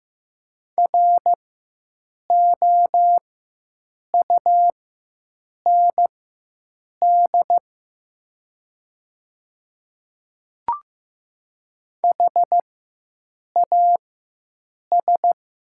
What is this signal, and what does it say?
Morse code, heard as a keyed tone, about 15 wpm, spaced as for 7 wpm, 700 Hz. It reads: ROUND HAS
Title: 4x Character Spacing - Top 300 Words - 15wpm